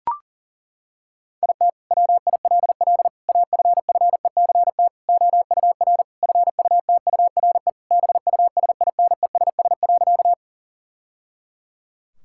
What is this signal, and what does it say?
Morse code, heard as a keyed tone, about 40 wpm, 700 Hz. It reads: IT WILL AFFECT OUR FUTURE BUSINESS.
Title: Sentences from Top 1000 Words - Review Part 6 - 40wpm